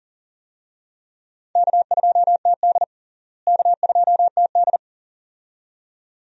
Morse code, heard as a keyed tone, about 40 wpm, 700 Hz. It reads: K2TD K2TD
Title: USA Call Signs - 4 Characters - Repeated 2x - 40wpm